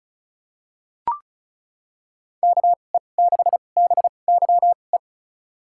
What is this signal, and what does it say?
Morse code, heard as a keyed tone, about 34 wpm, spaced as for 25 wpm, 700 Hz. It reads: KE6BYE
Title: USA Call Signs - 6 Characters - 2x Character Spacing - 35wpm